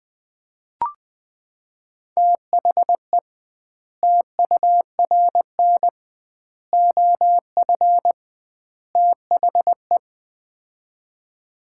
Morse code, heard as a keyed tone, about 21 wpm, 700 Hz. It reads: THE TURN OF THE
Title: 2x Word-Spacing - Sets of 4 Words - 20wpm